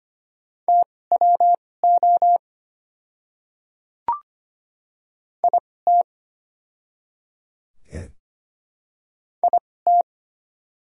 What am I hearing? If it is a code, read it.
Morse code, heard as a keyed tone, about 25 wpm, spaced as for 18 wpm, 690 Hz. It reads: TWO IT IT
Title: 2x Character Spacing - Top 100 Words - 25wpm